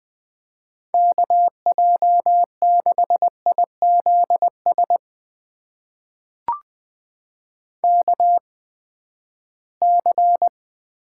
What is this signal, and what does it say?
Morse code, heard as a keyed tone, about 20 wpm, spaced as for 14 wpm, 700 Hz. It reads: KJ6IZS KC
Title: USA Call Signs - 8x Character Spacing - 20wpm